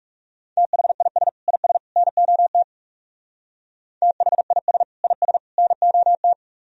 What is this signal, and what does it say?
Morse code, heard as a keyed tone, about 37 wpm, 700 Hz. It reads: THIS IS NOT THIS IS NOT
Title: Speed-Racing - Sets of 3 Words 30wpm